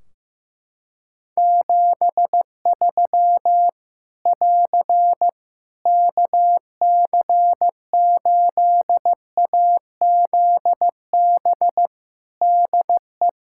Morse code, heard as a keyed tone, about 15 wpm, 700 Hz. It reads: 73 <AR> KC8AZB DE
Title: Practice QSO - Exchange 4 - Ending - 15wpm